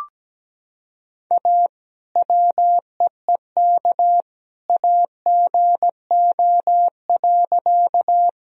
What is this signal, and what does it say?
Morse code, heard as a keyed tone, about 17 wpm, 700 Hz. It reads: A WEEK AGO.
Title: Sentences from Top 1000 Words - Review Part 3 - 17wpm